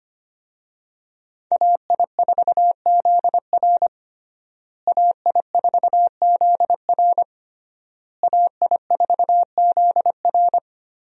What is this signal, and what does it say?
Morse code, heard as a keyed tone, about 25 wpm, 700 Hz. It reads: AI4ZR AI4ZR AI4ZR
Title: USA Call Signs - 5 Characters - Repeated 3x - 25wpm